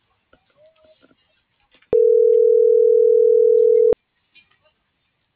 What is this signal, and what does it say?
Ringback tone (North America), 440 + 480 Hz.